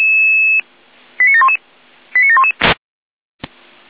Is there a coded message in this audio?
Yes: five-tone selective call (ZVEI) from 1.2 to 2.5 s, heard as quick runs of five tones, tones of about 70 ms. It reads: 89711 89711